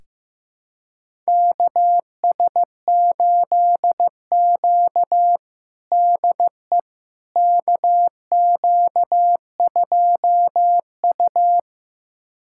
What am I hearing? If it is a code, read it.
Morse code, heard as a keyed tone, about 15 wpm, 700 Hz. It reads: KS8Q DE KQ2U
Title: USA Callsign DE Callsign - 4 Characters - 15wpm